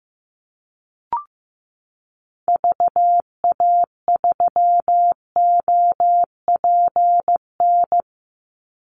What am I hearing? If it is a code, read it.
Morse code, heard as a keyed tone, about 15 wpm, 700 Hz. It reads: VA3OPN
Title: Call Signs Repeated 3x - 15wpm